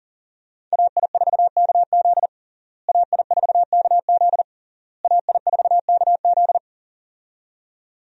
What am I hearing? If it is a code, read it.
Morse code, heard as a keyed tone, about 40 wpm, 700 Hz. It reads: AI4KZ AI4KZ AI4KZ
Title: USA Call Signs - 5 Characters - Repeated 3x - 40wpm